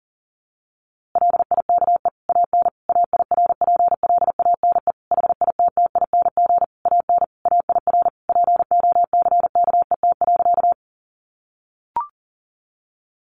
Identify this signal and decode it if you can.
Morse code, heard as a keyed tone, about 40 wpm, 700 Hz. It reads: LIKE AN AIRPLANE HITTING AN AIR POCKET.
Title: A Sound Of Thunder 40wpm